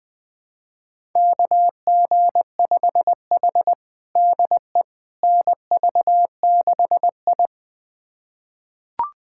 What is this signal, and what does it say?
Morse code, heard as a keyed tone, about 20 wpm, 700 Hz. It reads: KG5H DE NV6I
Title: USA Callsign DE Callsign - 4 Characters - 20wpm